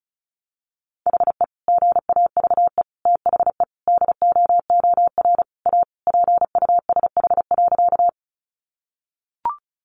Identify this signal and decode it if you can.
Morse code, heard as a keyed tone, about 34 wpm, 700 Hz. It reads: HE GAVE THE DOOR A PUSH.
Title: Sentences from Top 1300 Words 35wpm